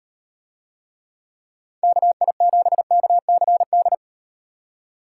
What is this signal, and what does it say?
Morse code, heard as a keyed tone, about 35 wpm, 700 Hz. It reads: KI7KCD